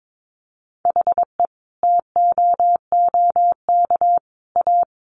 Morse code, heard as a keyed tone, about 22 wpm, 700 Hz. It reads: HE TOOK A
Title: Sentences from Top 1000 Words 22wpm